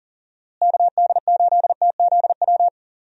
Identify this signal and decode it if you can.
Morse code, heard as a keyed tone, about 37 wpm, 700 Hz. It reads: KD8TZW
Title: USA Call Signs - 6 Characters - Repeated 3x - 40wpm